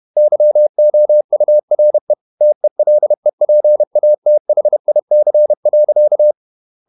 Morse code, heard as a keyed tone, about 31 wpm, 600 Hz. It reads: YOURE TELEPATHIC.
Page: Never 31wpm | CW med Gnister